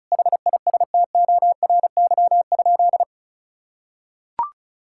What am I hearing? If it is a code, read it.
Morse code, heard as a keyed tone, about 36 wpm, 700 Hz. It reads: HISTORY?
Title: Sentences from Top 1000 Words - Review Part 5 - 35wpm